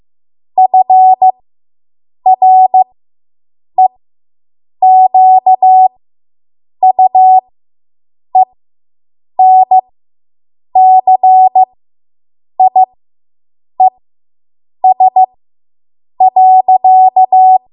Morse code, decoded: FREQUENCIES.